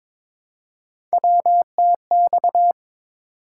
Morse code, heard as a keyed tone, about 22 wpm, 700 Hz. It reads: WTX
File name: practice / 22wpm / ARRL Sections v2 - 22wpm